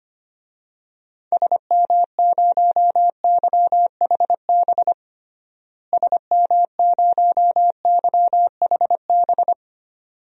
Morse code, heard as a keyed tone, about 25 wpm, 700 Hz. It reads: SM0YHB SM0YHB
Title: DXCC-100 Call Signs - 6 Characters - Repeated 2x - 25wpm